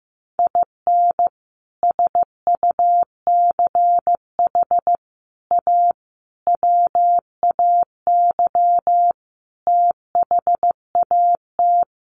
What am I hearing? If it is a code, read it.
Morse code, heard as a keyed tone, about 15 wpm, 700 Hz. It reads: IN SUCH A WAY THAT
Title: Sets of 5 Words 15wpm